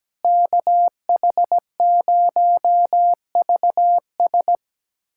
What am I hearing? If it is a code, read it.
Morse code, heard as a keyed tone, about 17 wpm, 700 Hz. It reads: KH0VS
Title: USA Callsign DE Callsign - 5 Characters - 17wpm